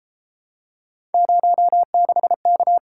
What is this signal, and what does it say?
Morse code, heard as a keyed tone, about 30 wpm, 700 Hz. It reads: 06K